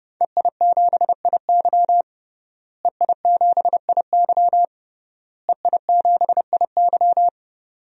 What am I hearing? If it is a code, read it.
Morse code, heard as a keyed tone, about 30 wpm, 700 Hz. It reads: EI7IY EI7IY EI7IY
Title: DXCC-100 Call Signs - 5 Characters - Repeated 3x - 30wpm